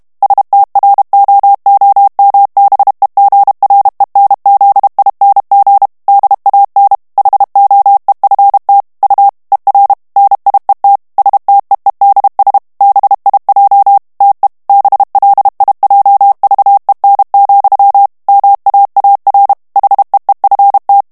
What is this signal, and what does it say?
Morse code, heard as a keyed tone, about 32 wpm, 800 Hz. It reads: STROOMBEGRENZING DAN HOEFT U ER NIET STEEDS BIJ TE BLIJVEN, MAAR HEEFT
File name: HOLLANDAIS texte clair 2 vitesse 30 mots minute